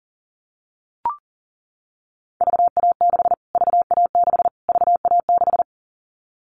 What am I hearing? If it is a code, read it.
Morse code, heard as a keyed tone, about 40 wpm, 700 Hz. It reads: VA6 VA6 VA6
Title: Callsign Prefixes for Top 25 DXCC - Repeated 3x - 40wpm